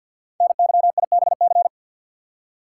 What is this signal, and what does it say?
Morse code, heard as a keyed tone, about 52 wpm, 700 Hz. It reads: NYIBC
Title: Code Groups - 5 Letter:Numbers - 50wpm